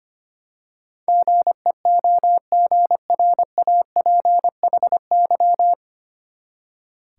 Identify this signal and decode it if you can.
Morse code, heard as a keyed tone, about 25 wpm, 700 Hz. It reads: GEOGRAPHY
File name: Nine Letter Words - 25wpm